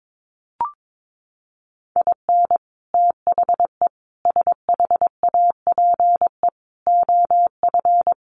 Morse code, heard as a keyed tone, about 22 wpm, 700 Hz. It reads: IN THE SHAPE OF
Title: Sets of 5 Words Encore 22wpm